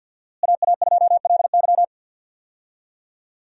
Morse code, heard as a keyed tone, about 52 wpm, 700 Hz. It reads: AA2LY
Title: USA Call Signs - 5 Characters - Repeated 3x - 50wpm